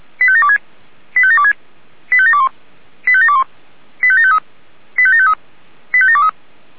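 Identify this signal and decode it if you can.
Five-tone selective call (ZVEI), heard as quick runs of five tones, tones of about 70 ms. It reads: 85726 85726 85721 85721 85753 85753 85723
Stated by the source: ZVEI five-tone call 85759 received on SDR.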